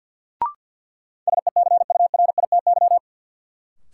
Morse code, heard as a keyed tone, about 50 wpm, 700 Hz. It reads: SECURITY
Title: Sentences from Top 1000 Words - Review Part 4 - New Word Drill - 50wpm